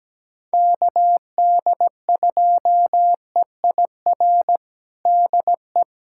Morse code, heard as a keyed tone, about 17 wpm, 700 Hz. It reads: KD2EIR DE
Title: USA Callsign DE Callsign - 6 Characters - 17wpm